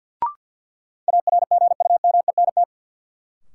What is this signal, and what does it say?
Morse code, heard as a keyed tone, about 50 wpm, 700 Hz. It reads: ARGUMENT
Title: Sentences from Top 1400 Words - New Word Drill Alt - 50wpm